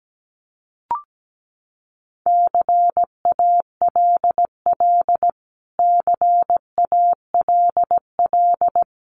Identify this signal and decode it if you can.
Morse code, heard as a keyed tone, about 17 wpm, 700 Hz. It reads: CALL CALL
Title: Top 200 Words Repeated 17wpm